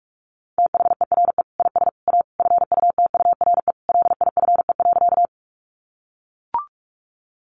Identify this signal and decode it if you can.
Morse code, heard as a keyed tone, about 44 wpm, 700 Hz. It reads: THERE IS A FUTURE LIFE.
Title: Sentences from Top 700 Words 45wpm